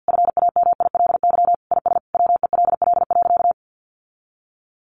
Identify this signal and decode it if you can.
Morse code, heard as a keyed tone, about 50 wpm, 700 Hz. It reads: FAMILY IS WELL.
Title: Sentences from Top 800 Words 50wpm